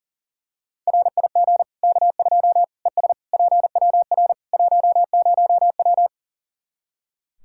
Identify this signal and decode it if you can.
Morse code, heard as a keyed tone, about 40 wpm, 700 Hz. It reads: RIG K2 ES PWR 10W
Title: Practice QSO - Exchange 2 - Info - 40wpm